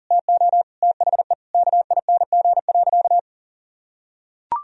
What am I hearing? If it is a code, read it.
Morse code, heard as a keyed tone, about 40 wpm, 700 Hz. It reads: TO THE KING.
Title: Sentences from Top 1000 Words - Review Part 4 - 40wpm